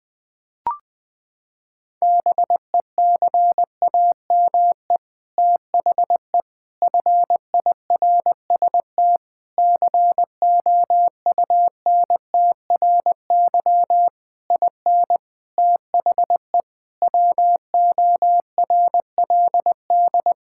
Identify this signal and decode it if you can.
Morse code, heard as a keyed tone, about 20 wpm, 700 Hz. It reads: BECAME THE FIRST COUNTRY IN THE WORLD
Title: Sets of 8 Words - 20wpm